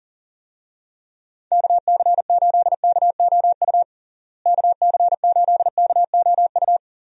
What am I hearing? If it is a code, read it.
Morse code, heard as a keyed tone, about 40 wpm, 700 Hz. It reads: KC8KOU KC8KOU
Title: USA Call Signs - 6 Characters - Repeated 3x - 40wpm